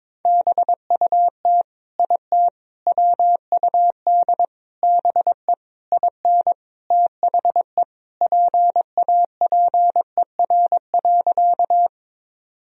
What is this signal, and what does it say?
Morse code, heard as a keyed tone, about 22 wpm, 700 Hz. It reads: BUT IT WUD BE IN THE PAPER.
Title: Sentences from Top 300 Words and Abbreviations 22wpm